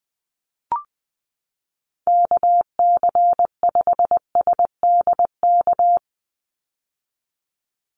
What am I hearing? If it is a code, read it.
Morse code, heard as a keyed tone, about 20 wpm, 700 Hz. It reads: KC5SDK
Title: USA Call Signs 20wpm